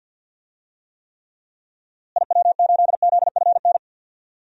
Morse code, heard as a keyed tone, about 46 wpm, 700 Hz. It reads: IW8ZFN
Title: DXCC-100 Call Signs - 6 Characters - Repeated 3x - 50wpm